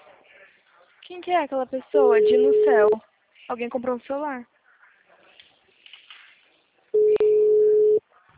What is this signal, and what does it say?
Ringback tone (Europe, 425 Hz), with speech.